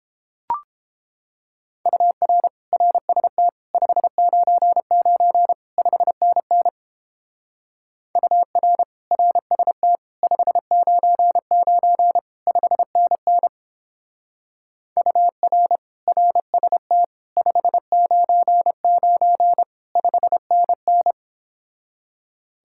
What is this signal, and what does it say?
Morse code, heard as a keyed tone, about 29 wpm, 700 Hz. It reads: UR RST 599 5NN UR RST 599 5NN UR RST 599 5NN